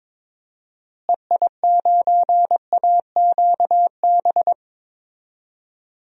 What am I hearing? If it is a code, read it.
Morse code, heard as a keyed tone, about 22 wpm, 700 Hz. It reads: EI9AQB
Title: DXCC-100 Call Signs - 6 Characters - Repeated 2x - 22wpm